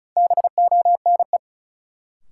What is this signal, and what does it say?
Morse code, heard as a keyed tone, about 36 wpm, 700 Hz. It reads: BONE
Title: Sentences from Top 1600 Words - New Word Drill Alt - 35wpm